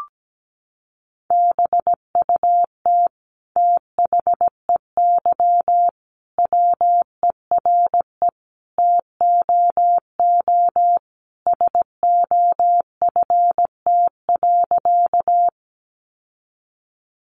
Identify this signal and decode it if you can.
Morse code, heard as a keyed tone, about 17 wpm, 700 Hz. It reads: BUT THEY WERE TOO SOFT.
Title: Sentences from Top 1500 Words 17wpm